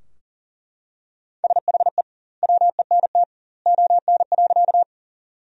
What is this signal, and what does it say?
Morse code, heard as a keyed tone, about 37 wpm, 700 Hz. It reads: SHE WENT ON.
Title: 2x Word-Spacing - Sentences from Top 200 Words - 40wpm